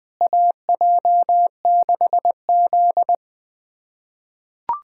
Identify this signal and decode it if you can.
Morse code, heard as a keyed tone, about 21 wpm, 700 Hz. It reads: AJ6Z
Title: USA Callsign DE Callsign - 4 Characters - 20wpm